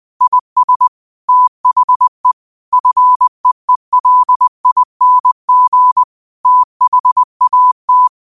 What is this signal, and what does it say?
Morse code, heard as a keyed tone, about 20 wpm, 1000 Hz. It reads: IS THE FEELING THAT